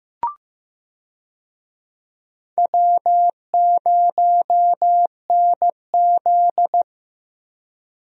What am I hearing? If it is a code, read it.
Morse code, heard as a keyed tone, about 15 wpm, 700 Hz. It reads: W0NZ
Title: USA Call Signs - 4 Characters - Repeated 3x - 15wpm